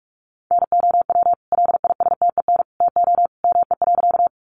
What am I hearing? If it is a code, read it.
Morse code, heard as a keyed tone, about 44 wpm, 700 Hz. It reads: NOW LISTEN TO ME.
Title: Sentences from Top 400 Words 45wpm